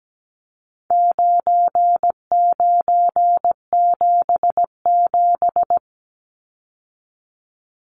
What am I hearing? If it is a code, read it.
Morse code, heard as a keyed tone, about 17 wpm, 700 Hz. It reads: 9977
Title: Numbers - 4-Digit 17wpm